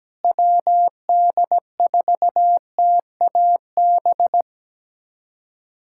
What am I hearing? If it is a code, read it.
Morse code, heard as a keyed tone, about 17 wpm, 700 Hz. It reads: WD4TAB
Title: USA Callsign DE Callsign - 6 Characters - 17wpm